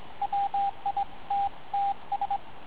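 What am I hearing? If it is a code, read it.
Morse code, heard as a keyed tone, about 22 wpm, 800 Hz, with radio noise. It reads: WITTS
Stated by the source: Morse code contact made with a crystal-controlled transceiver, a long chatty CW conversation early in the morning.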